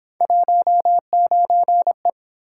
Morse code, heard as a keyed tone, about 28 wpm, 700 Hz. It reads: 19E